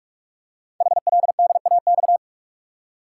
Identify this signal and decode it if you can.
Morse code, heard as a keyed tone, about 44 wpm, 700 Hz. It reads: HLDAX
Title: Code Groups - 5 Letters - 45wpm